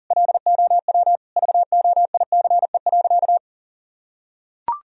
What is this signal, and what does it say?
Morse code, heard as a keyed tone, about 40 wpm, 700 Hz. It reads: LOW VOICE.
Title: Sentences from Top 1000 Words - Review Part 1 - 40wpm